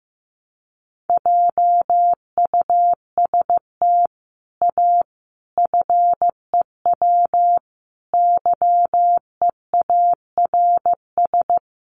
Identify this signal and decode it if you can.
Morse code, heard as a keyed tone, about 15 wpm, 700 Hz. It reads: JUST A FEW YEARS
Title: Sets of 5 Words Encore 15wpm